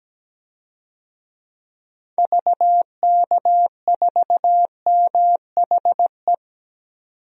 Morse code, heard as a keyed tone, about 17 wpm, 700 Hz. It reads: VK4MHE
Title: DXCC-100 Call Signs - 6 Characters - 17wpm